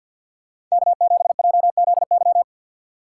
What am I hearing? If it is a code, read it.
Morse code, heard as a keyed tone, about 53 wpm, 700 Hz. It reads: K7JZY
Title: USA Call Signs - 5 Characters - 6x Character Spacing - 50wpm